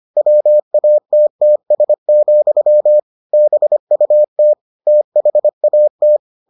Morse code, heard as a keyed tone, about 25 wpm, 600 Hz. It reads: WATTS, BUT THAT